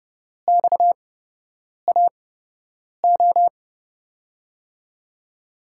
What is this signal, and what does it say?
Morse code, heard as a keyed tone, about 30 wpm, spaced as for 8 wpm, 700 Hz. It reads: XAO
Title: USA Call Signs - 5 Characters - 8x Character Spacing - 30wpm